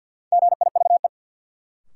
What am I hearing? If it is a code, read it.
Morse code, heard as a keyed tone, about 53 wpm, 700 Hz. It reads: GIVE
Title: Sentences from Top 200 Words - New Word Drill - 50wpm